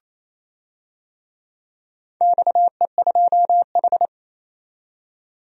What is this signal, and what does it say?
Morse code, heard as a keyed tone, about 27 wpm, 700 Hz. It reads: XE2H
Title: DXCC-100 Call Signs - 4 Characters - 28wpm